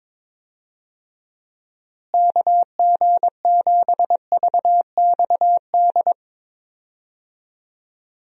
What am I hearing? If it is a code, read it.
Morse code, heard as a keyed tone, about 21 wpm, 700 Hz. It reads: KG7VXD
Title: USA Call Signs - 6 Characters - Repeated 3x - 22wpm